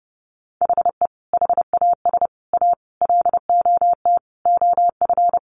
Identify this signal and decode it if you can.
Morse code, heard as a keyed tone, about 30 wpm, 700 Hz. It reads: HE HAS A LOT OF
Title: Additional Easy Phrases 30wpm